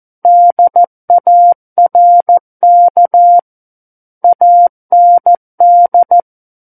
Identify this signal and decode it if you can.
Morse code, heard as a keyed tone, about 14 wpm, 700 Hz. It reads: DARK AND
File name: Book One / 05-Chapter 5 - 14 WPM